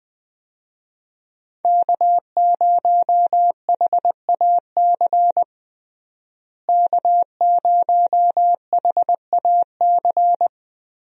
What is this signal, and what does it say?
Morse code, heard as a keyed tone, about 20 wpm, 700 Hz. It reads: K0HAC K0HAC
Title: USA Call Signs - 5 Characters - Repeated 3x - 20wpm